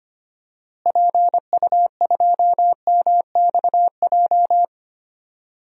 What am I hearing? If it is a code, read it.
Morse code, heard as a keyed tone, about 25 wpm, 700 Hz. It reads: PU2MXJ
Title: DXCC-100 Call Signs - 6 Characters - Repeated 2x - 25wpm